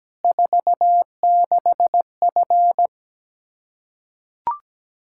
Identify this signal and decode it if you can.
Morse code, heard as a keyed tone, about 17 wpm, 700 Hz. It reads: 46F
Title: Practice QSO - Exchange 2 - Info - 17wpm